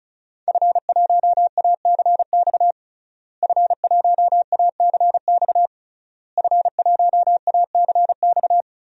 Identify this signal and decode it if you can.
Morse code, heard as a keyed tone, about 36 wpm, 700 Hz. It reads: F1ACX F1ACX F1ACX
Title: DXCC-100 Call Signs - 5 Characters - Repeated 3x - 35wpm